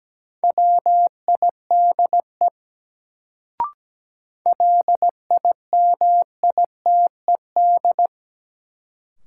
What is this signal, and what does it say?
Morse code, heard as a keyed tone, about 17 wpm, 700 Hz. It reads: WIDE LIMITED
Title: Sentences from Top 900 Words - New Word Drill - 17wpm